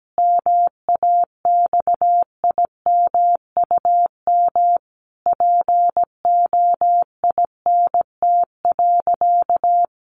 Morse code, heard as a keyed tone, about 17 wpm, 700 Hz. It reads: MAXIMUM POINT.
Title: Sentences from Top 1300 Words 17wpm